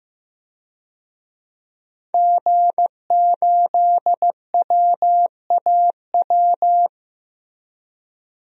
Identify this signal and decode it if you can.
Morse code, heard as a keyed tone, about 15 wpm, 700 Hz. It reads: G8WAW
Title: DXCC-100 Call Signs - 5 Characters - Repeated 3x - 15wpm